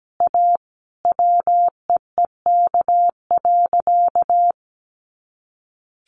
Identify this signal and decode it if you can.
Morse code, heard as a keyed tone, about 17 wpm, 700 Hz. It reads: A WEEK.
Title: Easy Phrases 17wpm